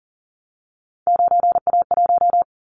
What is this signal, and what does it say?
Morse code, heard as a keyed tone, about 38 wpm, 700 Hz. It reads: 9A1
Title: Callsign Prefixes for Top 25 DXCC - 40wpm